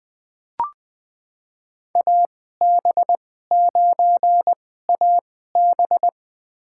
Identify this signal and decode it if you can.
Morse code, heard as a keyed tone, about 20 wpm, spaced as for 14 wpm, 700 Hz. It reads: AB9AB
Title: USA Call Signs - 5 Characters - 2x Character Spacing - 20wpm